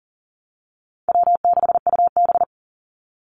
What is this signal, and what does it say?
Morse code, heard as a keyed tone, about 40 wpm, 700 Hz. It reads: W6UB
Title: NA SOTA Chaser Callsigns 40wpm